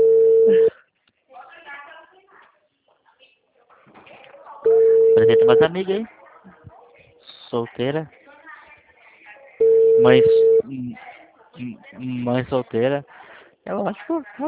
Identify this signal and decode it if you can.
Ringback tone (China, 450 Hz), with speech.
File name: out- -2010- - - .17568